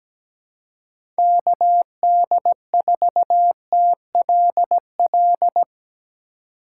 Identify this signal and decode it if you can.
Morse code, heard as a keyed tone, about 17 wpm, 700 Hz. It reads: KD4TLL
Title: USA Call Signs - 4x Character Spacing - 17wpm